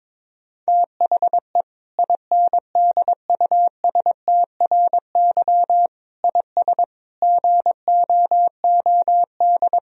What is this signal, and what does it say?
Morse code, heard as a keyed tone, about 22 wpm, 700 Hz. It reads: THE INDUSTRY IS GOOD
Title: Sentences from Top 1000 Words - Review Part 4 - 22wpm